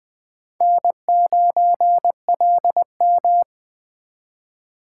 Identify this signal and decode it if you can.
Morse code, heard as a keyed tone, about 21 wpm, 700 Hz. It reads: N9LM